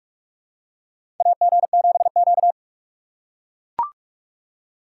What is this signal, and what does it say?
Morse code, heard as a keyed tone, about 44 wpm, 700 Hz. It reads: AG7Q
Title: USA Call Signs - 3x Character Spacing - 45wpm